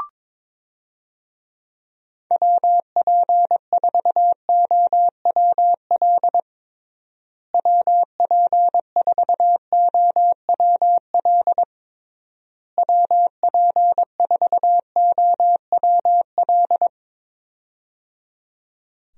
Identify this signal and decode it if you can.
Morse code, heard as a keyed tone, about 21 wpm, 700 Hz. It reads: WP4OWL WP4OWL WP4OWL
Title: USA Call Signs - 6 Characters - Repeated 3x - 22wpm